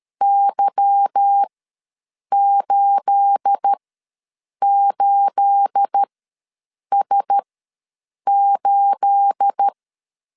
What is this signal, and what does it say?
Morse code, heard as a keyed tone, about 13 wpm, spaced as for 7 wpm, 800 Hz. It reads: Y88S8
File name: LEZIONI / 9 28 ws 12wpm 3354